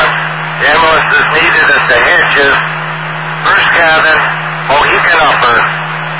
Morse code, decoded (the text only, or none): none